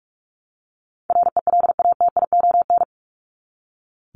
Morse code, heard as a keyed tone, about 44 wpm, 700 Hz. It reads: RELATION
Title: ES:S Suffix - Top 100 Words with Base Words 45wpm